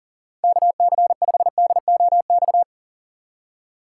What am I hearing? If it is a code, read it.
Morse code, heard as a keyed tone, about 40 wpm, 700 Hz. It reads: KC5DOX
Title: USA Call Signs - 6 Characters - 3x Character Spacing - 40wpm